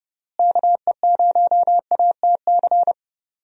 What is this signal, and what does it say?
Morse code, heard as a keyed tone, about 30 wpm, 700 Hz. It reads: KE0ATC
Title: USA Call Signs - 6 Characters - Repeated 3x - 30wpm